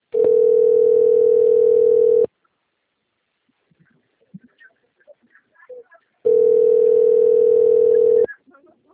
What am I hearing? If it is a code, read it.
Ringback tone (North America), 440 + 480 Hz.